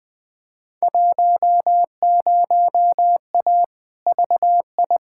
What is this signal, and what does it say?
Morse code, heard as a keyed tone, about 20 wpm, 700 Hz. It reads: 10A VI
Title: Field Day Exchanges Encore - 20wpm